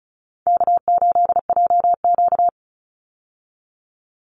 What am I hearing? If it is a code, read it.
Morse code, heard as a keyed tone, about 34 wpm, 700 Hz. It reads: K8JQ
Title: Easy Call Signs Repeated 3x - 35wpm